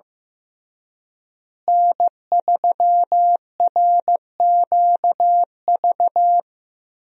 Morse code, heard as a keyed tone, about 15 wpm, 700 Hz. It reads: N3RQV
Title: USA Call Signs - 5 Characters - Repeated 3x - 15wpm